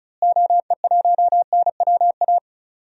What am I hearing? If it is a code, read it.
Morse code, heard as a keyed tone, about 34 wpm, 700 Hz. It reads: OE1NWA
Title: DXCC-100 Call Signs - 6 Characters - Repeated 2x - 35wpm